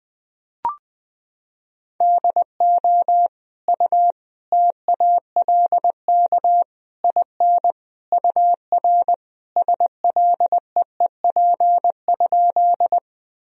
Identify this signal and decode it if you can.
Morse code, heard as a keyed tone, about 20 wpm, 700 Hz. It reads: DO U TALK IN UR SLEEP?
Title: Sentences from Top 500 Words and Abbreviations 20wpm